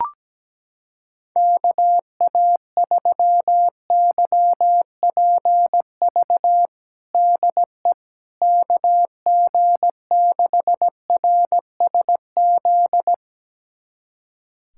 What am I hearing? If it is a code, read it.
Morse code, heard as a keyed tone, about 17 wpm, 700 Hz. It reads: KA3YPV DE KG6RSZ
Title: USA Callsign DE Callsign 17wpm